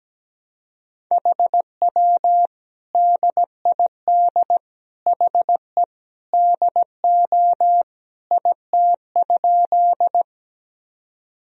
Morse code, heard as a keyed tone, about 17 wpm, 700 Hz. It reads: HW DID HE DO IT?
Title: Sentences from Top 100 Words and Abbreviations 17wpm